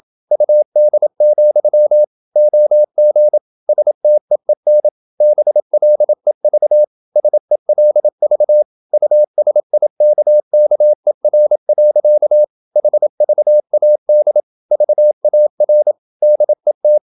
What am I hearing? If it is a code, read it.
Morse code, heard as a keyed tone, about 27 wpm, 600 Hz. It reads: UD, OG STEEN BLEV SELV USIKKER. HVAD VAR DET
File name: Ferie QRQ 27wpm